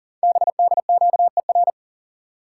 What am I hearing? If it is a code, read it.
Morse code, heard as a keyed tone, about 40 wpm, 700 Hz. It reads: BDQER
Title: Code Groups - 5 Letters - 40wpm